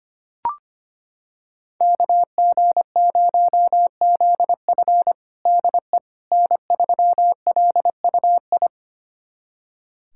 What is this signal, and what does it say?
Morse code, heard as a keyed tone, about 25 wpm, 700 Hz. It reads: KG0ZF DE N3LUI
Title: USA Callsign DE Callsign - 5 Characters - 25wpm